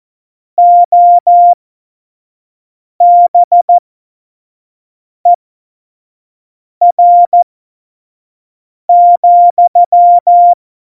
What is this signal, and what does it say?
Morse code, heard as a keyed tone, about 14 wpm, spaced as for 5 wpm, 700 Hz. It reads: OBER,
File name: Morsekurs / Witze / Witz 27. August 2019, 25 F70